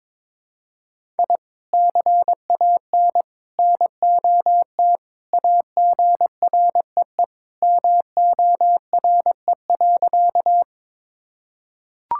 Morse code, heard as a keyed tone, about 22 wpm, 700 Hz. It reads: I CAN NOT AGREE MORE.
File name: practice / 22wpm / Sentences with 3-Word Phrases - 22wpm